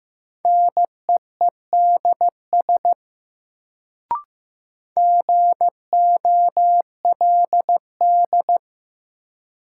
Morse code, heard as a keyed tone, about 15 wpm, 700 Hz. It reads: NEEDS GOLD
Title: Sentences from Top 1000 Words - Review Part 3 - New Word Drill - 15wpm